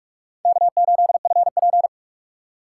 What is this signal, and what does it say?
Morse code, heard as a keyed tone, about 45 wpm, 700 Hz. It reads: K8FP
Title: USA Call Signs - 4 Characters - 6x Character Spacing - 45wpm